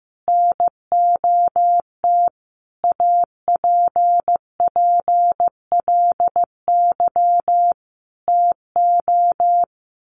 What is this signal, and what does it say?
Morse code, heard as a keyed tone, about 15 wpm, 700 Hz. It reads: NOT APPLY TO
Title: Sentences from Top 1200 Words 15wpm